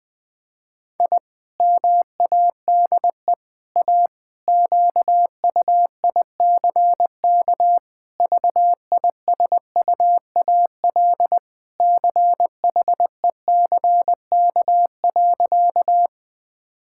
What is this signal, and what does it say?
Morse code, heard as a keyed tone, about 20 wpm, 700 Hz. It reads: I MADE A QUICK VISUAL CHECK.
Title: Sentences from Top 1000 Words - Review Part 6 - 20wpm